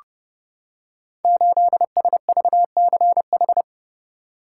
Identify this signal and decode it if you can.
Morse code, heard as a keyed tone, about 30 wpm, 700 Hz. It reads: 8SVCH
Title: Code Groups - 5 Letter:Numbers - 30wpm